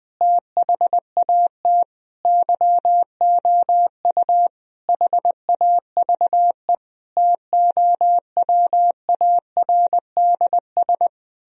Morse code, heard as a keyed tone, about 20 wpm, 700 Hz. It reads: THAT YOU HAVE TOWARDS
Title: Sentences from Top 1000 Words - Review Part 4 - 20wpm